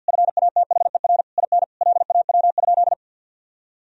Morse code, heard as a keyed tone, about 50 wpm, 700 Hz. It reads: FATHER IN LAW?
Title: Sentences from Top 1000 Words - Review Part 4 - 50wpm